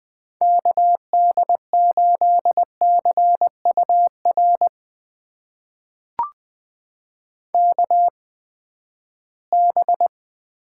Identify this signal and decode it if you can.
Morse code, heard as a keyed tone, about 20 wpm, spaced as for 14 wpm, 700 Hz. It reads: KD8CUR KB
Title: USA Call Signs - 6 Characters - 8x Character Spacing - 20wpm